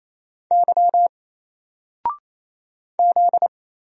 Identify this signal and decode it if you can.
Morse code, heard as a keyed tone, about 28 wpm, 700 Hz. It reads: Y Z
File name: Single-Letter-Number-28wpm